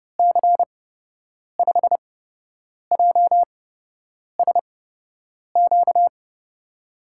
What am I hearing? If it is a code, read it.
Morse code, heard as a keyed tone, about 31 wpm, spaced as for 8 wpm, 700 Hz. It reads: C5JSQ
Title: USA Call Signs - 8x Character Spacing - 30wpm